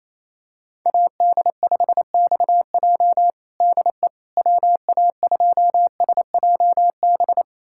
Morse code, heard as a keyed tone, about 28 wpm, 700 Hz. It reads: AD5XJ DE WA2SJB
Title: USA Callsign DE Callsign 28wpm